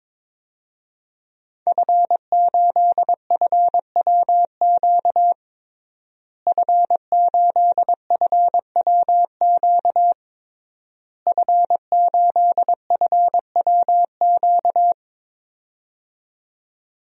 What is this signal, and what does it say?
Morse code, heard as a keyed tone, about 22 wpm, 700 Hz. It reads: F8FWQ F8FWQ F8FWQ
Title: DXCC-100 Call Signs - 5 Characters - Repeated 3x - 22wpm